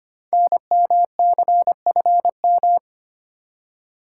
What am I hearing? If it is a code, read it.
Morse code, heard as a keyed tone, about 25 wpm, 700 Hz. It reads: NMCFM
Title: Code Groups - 5 Letters - 25wpm